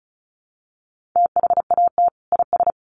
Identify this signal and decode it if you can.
Morse code, heard as a keyed tone, about 34 wpm, 700 Hz. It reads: THAT IS
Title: Sentences from Top 900 Words 35wpm